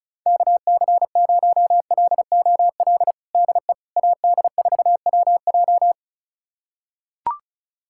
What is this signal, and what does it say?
Morse code, heard as a keyed tone, about 34 wpm, 700 Hz. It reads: KC0LOL DE AD4WJ
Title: USA Callsign DE Callsign 35wpm